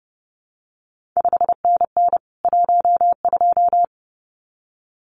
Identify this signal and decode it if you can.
Morse code, heard as a keyed tone, about 30 wpm, 700 Hz. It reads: 5NN 12
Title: CQ WW DX Contest Exchange Encore - 30wpm